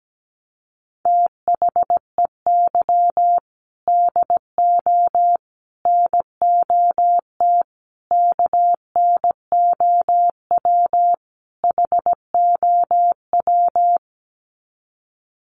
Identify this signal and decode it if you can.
Morse code, heard as a keyed tone, about 17 wpm, 700 Hz. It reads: THEY DO NOT KNOW HOW
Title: Sets of 5 Words Encore2 17wpm